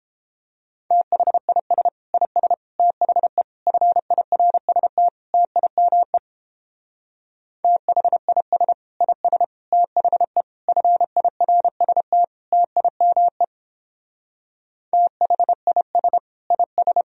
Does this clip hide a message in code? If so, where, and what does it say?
Yes: Morse code from 0.9 to 17.0 s, heard as a keyed tone, about 31 wpm, 700 Hz. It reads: THIS IS THE FIRST TIME THIS IS THE FIRST TIME THIS IS